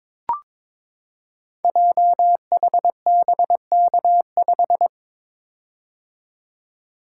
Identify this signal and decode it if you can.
Morse code, heard as a keyed tone, about 22 wpm, 700 Hz. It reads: JHBK5
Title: Code Groups - 5 Letter:Numbers - 22wpm